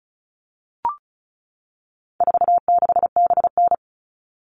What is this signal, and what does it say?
Morse code, heard as a keyed tone, about 34 wpm, 700 Hz. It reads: 46BN
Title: All Cut Numbers - 4-Digit 35wpm